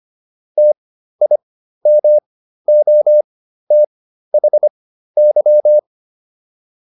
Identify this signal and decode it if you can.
Morse code, heard as a keyed tone, about 25 wpm, spaced as for 13 wpm, 600 Hz. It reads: TIMOTHY